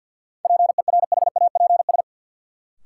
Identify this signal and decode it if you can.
Morse code, heard as a keyed tone, about 50 wpm, 700 Hz. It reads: PERHAPS
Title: Sentences from Top 500 Words - New Word Drill - 50wpm